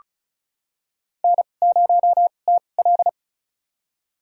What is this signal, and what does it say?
Morse code, heard as a keyed tone, about 36 wpm, spaced as for 25 wpm, 700 Hz. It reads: N0TL
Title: USA Call Signs - 4 Characters - 2x Character Spacing - 35wpm